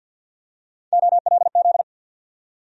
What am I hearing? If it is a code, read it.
Morse code, heard as a keyed tone, about 53 wpm, 700 Hz. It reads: OLZ
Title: USA Call Sign Postfixes - 3 Letters 50wpm